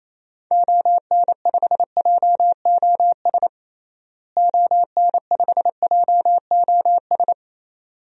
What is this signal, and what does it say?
Morse code, heard as a keyed tone, about 27 wpm, 700 Hz. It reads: ON5JOS ON5JOS
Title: DXCC-100 Call Signs - 6 Characters - Repeated 2x - 28wpm